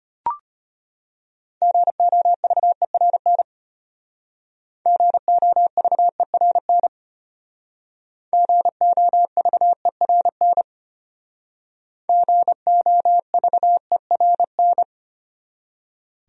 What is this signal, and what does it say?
Morse code, heard as a keyed tone, about 32 wpm, 700 Hz. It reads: GOVERN GOVERN GOVERN GOVERN